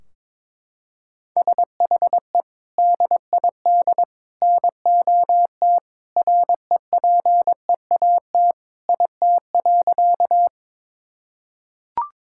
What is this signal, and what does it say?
Morse code, heard as a keyed tone, about 22 wpm, 700 Hz. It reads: SHE DID NOT REPEAT IT.
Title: Sentences from Top 1600 Words 22wpm